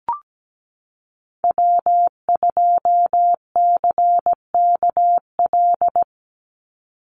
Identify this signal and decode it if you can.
Morse code, heard as a keyed tone, about 17 wpm, 700 Hz. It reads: W2CKL
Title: NA SOTA Activator Callsigns 17wpm